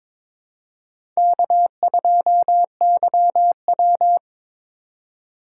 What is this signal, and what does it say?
Morse code, heard as a keyed tone, about 22 wpm, 700 Hz. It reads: K2YW
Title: USA Call Signs - 4 Characters - 3x Character Spacing - 22wpm